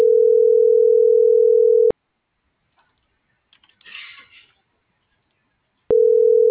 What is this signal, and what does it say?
Ringback tone (North America), 440 + 480 Hz.
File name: CENTRAL GOMSA / out lizeth.morales